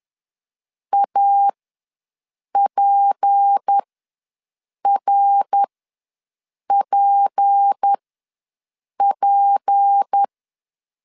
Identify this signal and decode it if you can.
Morse code, heard as a keyed tone, about 10 wpm, spaced as for 6 wpm, 800 Hz. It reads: APRPP
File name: LEZIONI / 6 PKX0 / 6 P ws 10wpm 3751